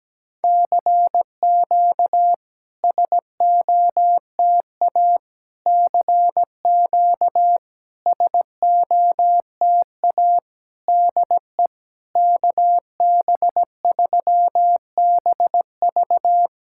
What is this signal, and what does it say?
Morse code, heard as a keyed tone, about 17 wpm, 700 Hz. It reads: CQ SOTA CQ SOTA DE KB3BV
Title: SOTA QSO Practice 17wpm